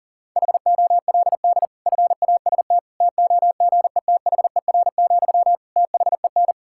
Morse code, heard as a keyed tone, about 40 wpm, 700 Hz. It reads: HOLD FAST TOGETHER, THEN
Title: Sentences from Top 1600 Words 40wpm